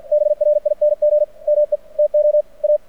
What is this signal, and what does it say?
Morse code, heard as a keyed tone, about 53 wpm, 600 Hz. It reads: BRING ME TO A
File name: archived / CRSnet